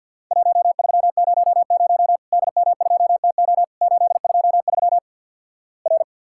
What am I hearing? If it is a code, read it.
Morse code, heard as a keyed tone, about 50 wpm, 700 Hz. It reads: 1300 DM2TO 823 R
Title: Worked All Europe DX Contest - QTC Exchange - 50wpm